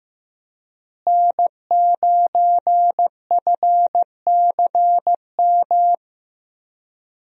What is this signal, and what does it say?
Morse code, heard as a keyed tone, about 15 wpm, 700 Hz. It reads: N9FCM
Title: USA Call Signs - 5 Characters - Repeated 2x - 15wpm